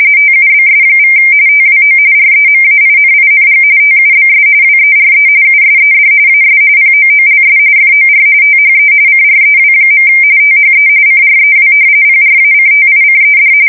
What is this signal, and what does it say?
RTTY, about 45.45 baud. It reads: TUBS= TWO REMAINING TUBS WERE USED AS POOLS FOR THE CREW. THE CREW PRACTICED
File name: fsk170 / battleshipnj